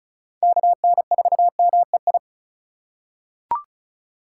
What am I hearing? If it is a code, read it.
Morse code, heard as a keyed tone, about 36 wpm, 700 Hz. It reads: KN4MEI
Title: USA Call Signs - 6 Characters - 3x Character Spacing - 35wpm